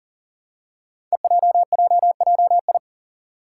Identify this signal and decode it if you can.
Morse code, heard as a keyed tone, about 41 wpm, 700 Hz. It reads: EJJJI
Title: Five Challenging Letter-Number Combinations 40wpm